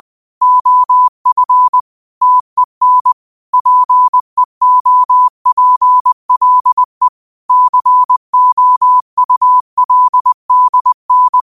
Morse code, heard as a keyed tone, about 20 wpm, 1000 Hz. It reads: OF TEN PEOPLE COULDN
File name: qotdfiles / 20wpm